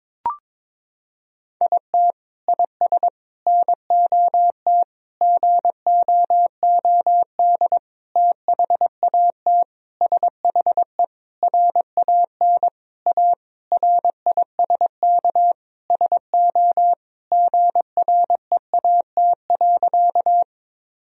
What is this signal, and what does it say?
Morse code, heard as a keyed tone, about 22 wpm, 700 Hz. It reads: IT IS NOT GOOD THAT SHE RAN A RISK SO GREAT.
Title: Sentences from Top 1000 Words - Review Part 4 - 22wpm